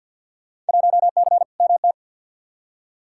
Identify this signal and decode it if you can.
Morse code, heard as a keyed tone, about 46 wpm, 700 Hz. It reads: 1C MT
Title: Field Day Exchanges Encore - 50wpm